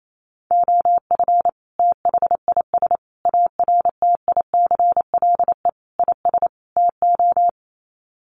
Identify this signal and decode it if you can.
Morse code, heard as a keyed tone, about 28 wpm, 700 Hz. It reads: OF THIS ARTICLE IS TO
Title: Sets of 5 Words Encore2 28wpm